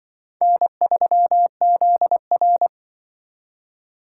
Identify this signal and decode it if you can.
Morse code, heard as a keyed tone, about 25 wpm, 700 Hz. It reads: N3ZR